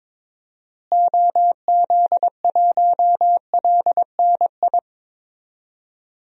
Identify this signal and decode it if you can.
Morse code, heard as a keyed tone, about 21 wpm, 700 Hz. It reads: OZ1LNI
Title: DXCC-100 Call Signs - 6 Characters - Repeated 3x - 22wpm